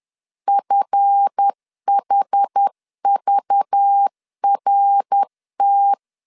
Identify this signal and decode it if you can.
Morse code, heard as a keyed tone, about 10 wpm, 800 Hz. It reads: FHVRT
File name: LEZIONI / 7 GF9Z / 7 GF 10wpm 2514